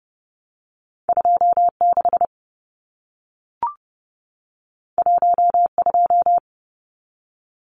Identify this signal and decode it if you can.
Morse code, heard as a keyed tone, about 30 wpm, 700 Hz. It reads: 26 12
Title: Numbers - 2-Digit 30wpm